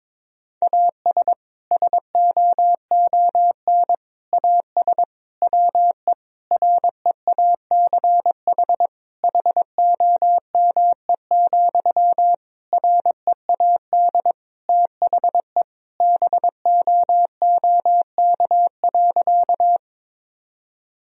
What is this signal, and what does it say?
Morse code, heard as a keyed tone, about 22 wpm, 700 Hz. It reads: AS SOON AS WE REACH HOME, READ THE BOOK.
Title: Sentences from Top 1000 Words - Review Part 2 - 22wpm